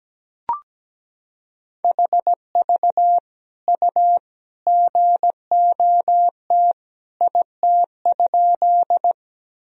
Morse code, heard as a keyed tone, about 17 wpm, 700 Hz. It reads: HV U GOT IT?
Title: Sentences from Top 300 Words and Abbreviations 17wpm